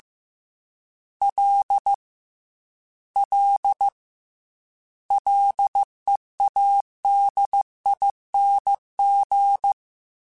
Morse code, heard as a keyed tone, about 15 wpm, 800 Hz. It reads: L L LEADING
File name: Ls / Single L m3s / Single L Prefixe 3x2 Twice @13